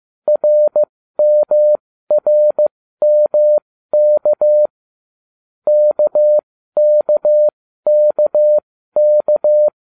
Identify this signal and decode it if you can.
Morse code, heard as a keyed tone, about 15 wpm, spaced as for 13 wpm, 600 Hz. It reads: RMRMK KKKK